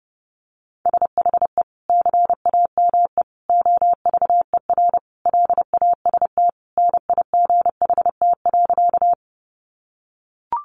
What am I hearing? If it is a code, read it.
Morse code, heard as a keyed tone, about 30 wpm, 700 Hz. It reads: SHE CAME OVER LAST NIGHT.
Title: Sentences from Top 300 Words 30wpm